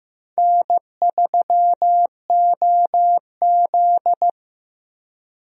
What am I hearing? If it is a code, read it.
Morse code, heard as a keyed tone, about 15 wpm, 700 Hz. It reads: N3OZ
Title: USA Call Signs - 4 Characters - Repeated 2x - 15wpm